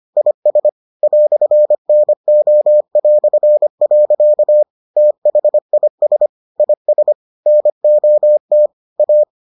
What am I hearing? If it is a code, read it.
Morse code, heard as a keyed tone, about 25 wpm, 600 Hz. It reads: IS "NO". THIS IS NOT A